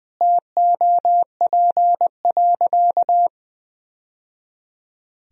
Morse code, heard as a keyed tone, about 20 wpm, 700 Hz. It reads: TOP.
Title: Sentences from Top 1000 Words - Review Part 2 - 20wpm